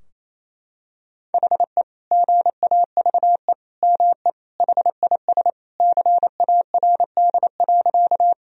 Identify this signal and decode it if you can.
Morse code, heard as a keyed tone, about 28 wpm, 700 Hz. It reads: HE GAVE ME HIS CARD.
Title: Sentences from Top 1000 Words - Review Part 4 - 28wpm